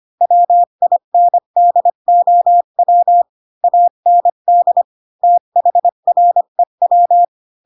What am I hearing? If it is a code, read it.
Morse code, heard as a keyed tone, about 25 wpm, 700 Hz. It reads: WINDOW AND THREW
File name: Pinocchio / Pinocchio ch7 20 25 ik1hge